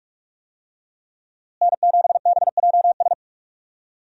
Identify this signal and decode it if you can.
Morse code, heard as a keyed tone, about 41 wpm, 700 Hz. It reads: N7BJS
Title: USA Call Signs - 5 Characters - Repeated 2x - 45wpm